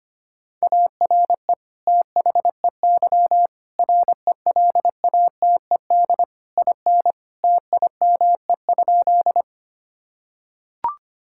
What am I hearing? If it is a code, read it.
Morse code, heard as a keyed tone, about 25 wpm, 700 Hz. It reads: ARE THEY RELATED IN TIME?
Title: Sentences from Top 1000 Words - Review Part 5 - 25wpm